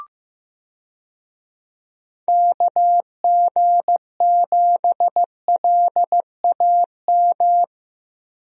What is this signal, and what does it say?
Morse code, heard as a keyed tone, about 15 wpm, 700 Hz. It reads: KG7LAM
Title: USA Call Signs - 6 Characters - Repeated 3x - 15wpm